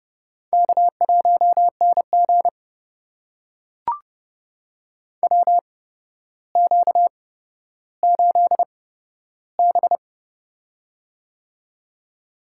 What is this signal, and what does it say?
Morse code, heard as a keyed tone, about 29 wpm, spaced as for 13 wpm, 700 Hz. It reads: K1NG WQ8B
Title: USA Call Signs - 4 Characters - 8x Character Spacing - 30wpm